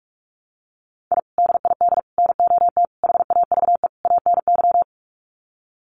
Morse code, heard as a keyed tone, about 44 wpm, 700 Hz. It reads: I DID NOT HAVE ANY
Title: Sets of 5 Words Encore 45wpm